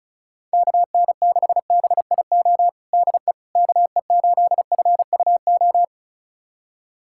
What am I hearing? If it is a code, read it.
Morse code, heard as a keyed tone, about 34 wpm, 700 Hz. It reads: KN6BIO DE KE8FUO
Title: USA Callsign DE Callsign - 6 Characters - 35wpm